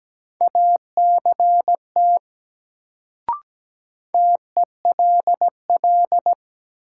Morse code, heard as a keyed tone, about 17 wpm, 700 Hz. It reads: ACT TELL
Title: Sentences from Top 200 Words - New Word Drill - 17wpm